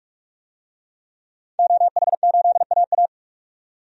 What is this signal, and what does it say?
Morse code, heard as a keyed tone, about 46 wpm, 700 Hz. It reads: OH8AA
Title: DXCC-100 Call Signs - 5 Characters - 45wpm